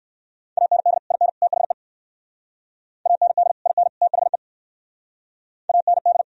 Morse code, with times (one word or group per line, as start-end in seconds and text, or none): 0.6-1.0 AND
1.1-1.3 IN
1.4-1.7 THE
3.1-3.5 AND
3.7-3.9 IN
4.0-4.4 THE
5.7-6.2 AND